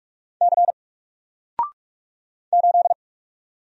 Morse code, heard as a keyed tone, about 44 wpm, 700 Hz. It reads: C 8
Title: Single Character (Letter:Number:Punctuation:Prosign) 45wpm